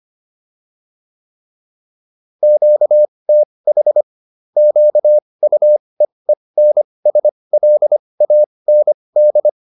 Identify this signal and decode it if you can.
Morse code, heard as a keyed tone, about 25 wpm, spaced as for 20 wpm, 600 Hz. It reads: QTH QUEENSLAND
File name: QSO407 20